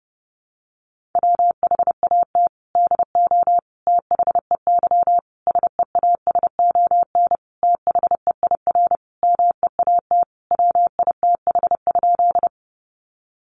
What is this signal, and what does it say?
Morse code, heard as a keyed tone, about 28 wpm, 700 Hz. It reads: WHAT DO THEY SEASON THEIR MEAT WITH?
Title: Sentences from Top 1400 Words 30wpm